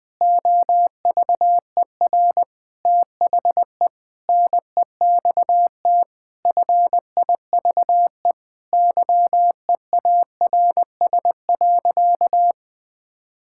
Morse code, heard as a keyed tone, about 20 wpm, 700 Hz. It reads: OVER THE NEXT FIVE YEARS.
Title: Sentences with 5-Word Phrases - Spoken Beginning - 20wpm